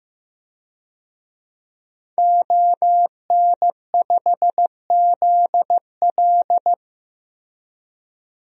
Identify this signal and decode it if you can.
Morse code, heard as a keyed tone, about 15 wpm, 700 Hz. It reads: ON5ZL
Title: DXCC-100 Call Signs - 5 Characters - Repeated 2x - 15wpm